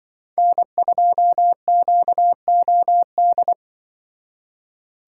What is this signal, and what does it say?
Morse code, heard as a keyed tone, about 24 wpm, 700 Hz. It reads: N2QOD
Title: Speed Racing - USA 1x3 Call Signs 20wpm